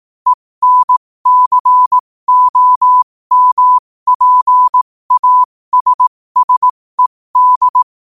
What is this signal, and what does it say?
Morse code, heard as a keyed tone, about 18 wpm, spaced as for 15 wpm, 1000 Hz. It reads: ENCOMPASSED
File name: qotdfiles / 15wpm